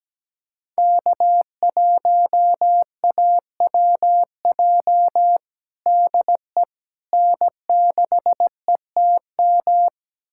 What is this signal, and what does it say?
Morse code, heard as a keyed tone, about 17 wpm, 700 Hz. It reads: K1AWJ DE N6ETM
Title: USA Callsign DE Callsign 17wpm